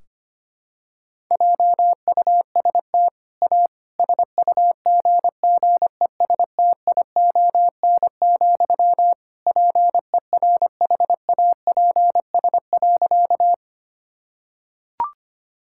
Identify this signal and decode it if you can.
Morse code, heard as a keyed tone, about 25 wpm, 700 Hz. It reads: JUST A SUGGESTION, PERHAPS.
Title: Sentences from Top 1600 Words 25wpm